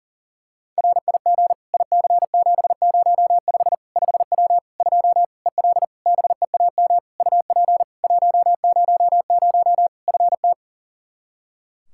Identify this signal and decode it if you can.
Morse code, heard as a keyed tone, about 40 wpm, 700 Hz. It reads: RIG IC705 5W 2 EL BEAM UP 100 FT
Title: K1USN SST QSO Practice 40wpm